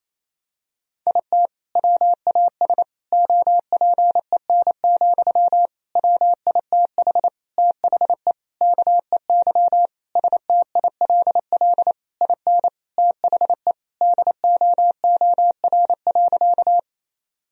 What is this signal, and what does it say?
Morse code, heard as a keyed tone, about 27 wpm, 700 Hz. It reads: IT WAS OPEN, WITH THE KEY STILL IN THE DOOR.
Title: Sentences from Top 1000 Words - Review Part 4 - 28wpm